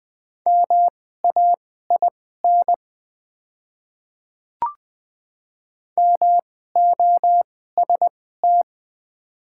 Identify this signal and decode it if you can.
Morse code, heard as a keyed tone, about 20 wpm, spaced as for 14 wpm, 700 Hz. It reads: MAIN MOST
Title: 2x Character Spacing - Top 300 Words - 20wpm